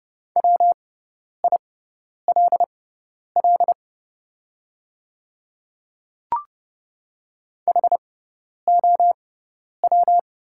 Morse code, heard as a keyed tone, about 30 wpm, spaced as for 10 wpm, 700 Hz. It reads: WILL HOW
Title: 6x Character Spacing - Top 100 Words - 30wpm